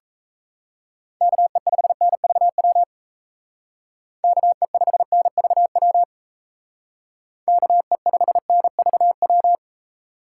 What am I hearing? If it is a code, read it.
Morse code, heard as a keyed tone, about 37 wpm, 700 Hz. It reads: KE5NVW KE5NVW KE5NVW